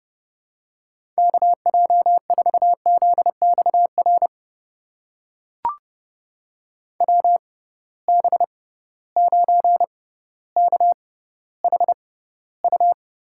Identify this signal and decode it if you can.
Morse code, heard as a keyed tone, about 30 wpm, spaced as for 15 wpm, 700 Hz. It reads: KJ4ZXR WB9KHU
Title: USA Call Signs - 6 Characters - 6x Character Spacing - 30wpm